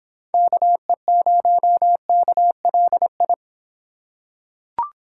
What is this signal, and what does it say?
Morse code, heard as a keyed tone, about 23 wpm, 700 Hz. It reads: KE0KLI